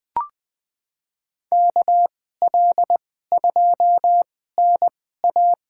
Morse code, heard as a keyed tone, about 20 wpm, spaced as for 14 wpm, 700 Hz. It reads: KL2NA
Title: USA Call Signs - 5 Characters - 2x Character Spacing - 20wpm